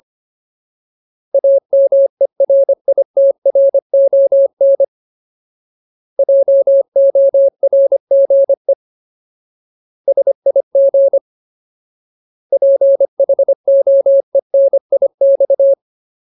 Morse code, heard as a keyed tone, about 25 wpm, 550 Hz. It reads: AMERITRON JORGE SIG PHOENIX
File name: Ultimate Word List / 25WPM Ultimate